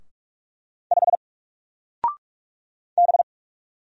Morse code, heard as a keyed tone, about 44 wpm, 700 Hz. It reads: 5 B
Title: Single Character (Letter:Number:Punctuation:Prosign) 45wpm